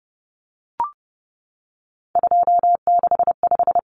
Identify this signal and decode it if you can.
Morse code, heard as a keyed tone, about 30 wpm, 700 Hz. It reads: 265
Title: Numbers - 3-Digit 30wpm